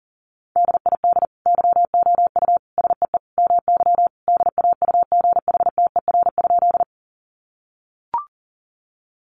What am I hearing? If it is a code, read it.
Morse code, heard as a keyed tone, about 40 wpm, 700 Hz. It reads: DID YOU SEE MY DAUGHTER?
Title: Sentences from Top 1500 Words 40wpm